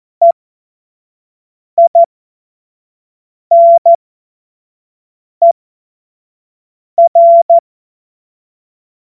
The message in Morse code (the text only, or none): EINER